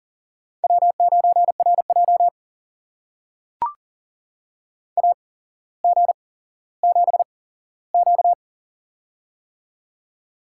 Morse code, heard as a keyed tone, about 40 wpm, spaced as for 17 wpm, 700 Hz. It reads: W9RJ AG7Q
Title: USA Call Signs - 4 Characters - 8x Character Spacing - 40wpm